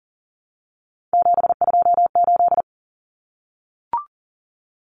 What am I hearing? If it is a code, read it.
Morse code, heard as a keyed tone, about 38 wpm, 700 Hz. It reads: 728
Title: Common Cut Numbers - 3-Digit 40wpm